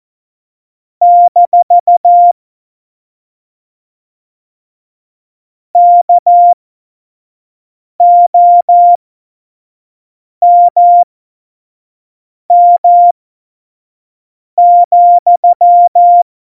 Morse code, heard as a keyed tone, about 14 wpm, spaced as for 5 wpm, 700 Hz. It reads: - KOMM,
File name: Morsekurs / Witze / Witz 18 August 2020, 25 F70